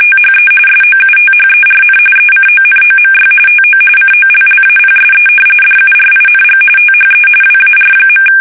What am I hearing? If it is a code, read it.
RTTY, about 45.45 baud. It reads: THE SOUTHERN ECONOMY TO COLLAPSE DURING THE WAR.
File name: fsk850 / usnavy3